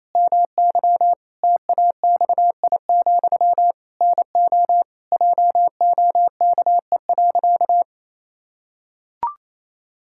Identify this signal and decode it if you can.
Morse code, heard as a keyed tone, about 28 wpm, 700 Hz. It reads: MY TAXI, NO JOKE.
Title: Pangram Challenge v2 - 28wpm